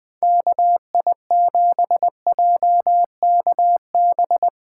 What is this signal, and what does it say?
Morse code, heard as a keyed tone, about 20 wpm, 700 Hz. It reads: KI7JKB